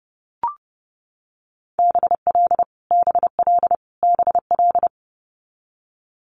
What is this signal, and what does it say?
Morse code, heard as a keyed tone, about 31 wpm, 690 Hz. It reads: BL BL BL
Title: Multi-letter Phonemes Repeated 3x 30wpm